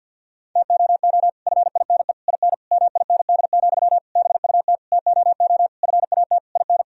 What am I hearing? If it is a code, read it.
Morse code, heard as a keyed tone, about 50 wpm, 700 Hz. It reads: TOO FINE IN MIND, BUT TOO FAT IN
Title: Sentences from Top 1000 Words - Review Part 4 - 50wpm